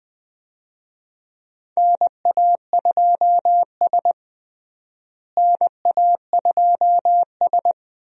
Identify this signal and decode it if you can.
Morse code, heard as a keyed tone, about 20 wpm, 700 Hz. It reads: NA2S NA2S
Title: USA Call Signs - 4 Characters - Repeated 3x - 20wpm